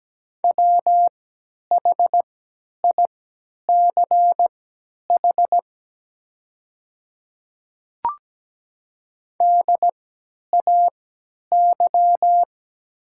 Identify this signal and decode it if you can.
Morse code, heard as a keyed tone, about 17 wpm, spaced as for 10 wpm, 700 Hz. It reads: WHICH DAY
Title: 3x Character Spacing - Top 100 Words - 17wpm